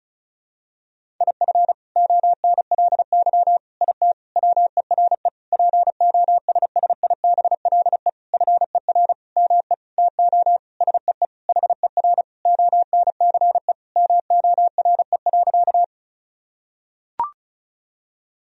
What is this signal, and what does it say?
Morse code, heard as a keyed tone, about 36 wpm, 700 Hz. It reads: IF ONLY IT WERE POSSIBLE FER ME TO SEE HER ONCE MORE.
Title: Sentences from Top 500 Words and Abbreviations 35wpm